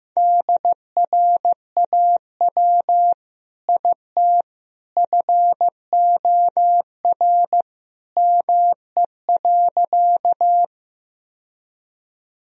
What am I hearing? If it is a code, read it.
Morse code, heard as a keyed tone, about 15 wpm, 700 Hz. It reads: DRAW IT FOR ME.
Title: Sentences from Top 1000 Words - Review Part 1 - 15wpm